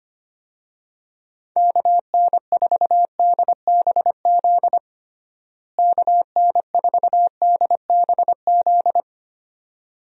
Morse code, heard as a keyed tone, about 25 wpm, 700 Hz. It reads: KN4DBZ KN4DBZ
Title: USA Call Signs - 6 Characters - Repeated 2x - 25wpm